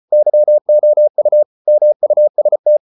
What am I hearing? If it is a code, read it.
Morse code, heard as a keyed tone, about 34 wpm, 600 Hz. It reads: YOU MUST